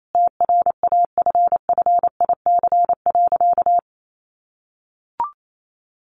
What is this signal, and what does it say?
Morse code, heard as a keyed tone, about 28 wpm, 700 Hz. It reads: TRAFFIC.
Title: Sentences from Top 1300 Words 28wpm